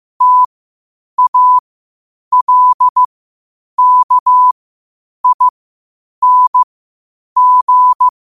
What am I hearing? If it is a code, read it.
Morse code, heard as a keyed tone, about 15 wpm, spaced as for 8 wpm, 1000 Hz. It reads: TALKING
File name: qotdfiles / 8wpm